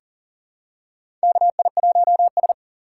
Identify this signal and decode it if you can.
Morse code, heard as a keyed tone, about 40 wpm, 700 Hz. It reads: KI1S
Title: USA Call Signs - 4 Characters - 2x Character Spacing - 40wpm